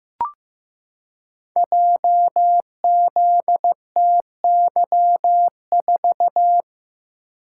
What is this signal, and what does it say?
Morse code, heard as a keyed tone, about 15 wpm, 700 Hz. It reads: JZTY4
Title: Code Groups - 5 Letter:Numbers - 15wpm